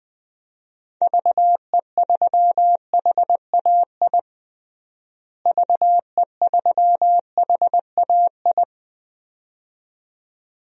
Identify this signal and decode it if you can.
Morse code, heard as a keyed tone, about 20 wpm, 700 Hz. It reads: VE3HAI VE3HAI
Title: DXCC-100 Call Signs - 6 Characters - Repeated 2x - 20wpm